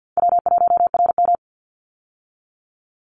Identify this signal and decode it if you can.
Morse code, heard as a keyed tone, about 50 wpm, 700 Hz. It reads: W1RM
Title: Easy Call Signs - 50wpm